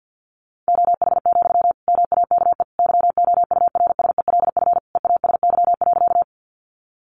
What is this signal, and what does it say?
Morse code, heard as a keyed tone, about 50 wpm, 700 Hz. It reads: OH, MAKE YOURSELF EASY.
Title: Sentences from Top 700 Words 50wpm